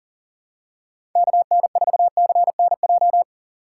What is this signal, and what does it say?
Morse code, heard as a keyed tone, about 40 wpm, 700 Hz. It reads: KN4CNJ
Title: USA Call Signs - 6 Characters - 4x Character Spacing - 40wpm